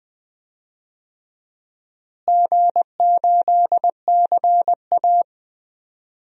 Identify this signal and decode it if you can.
Morse code, heard as a keyed tone, about 21 wpm, 700 Hz. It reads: G8CA
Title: DXCC-100 Call Signs - 4 Characters - Repeated 2x - 20wpm